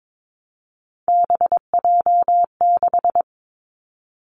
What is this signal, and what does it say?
Morse code, heard as a keyed tone, about 22 wpm, 700 Hz. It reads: BJ6
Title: Three Challenging Letter-Number Combinations 22wpm